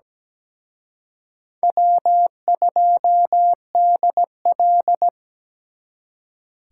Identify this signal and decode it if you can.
Morse code, heard as a keyed tone, about 20 wpm, 700 Hz. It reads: W2DL
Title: Speed Racing - USA 1x2 Call Signs 17wpm